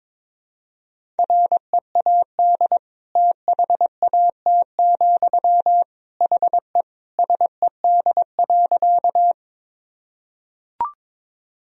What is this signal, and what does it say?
Morse code, heard as a keyed tone, about 21 wpm, 700 Hz. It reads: READ THAT, HE SED.
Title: Sentences from Top 200 Words and Abbreviations 22wpm